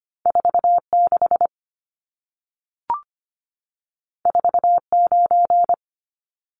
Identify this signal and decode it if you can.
Morse code, heard as a keyed tone, about 25 wpm, 700 Hz. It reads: 46 49
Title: Numbers - 2-Digit 25wpm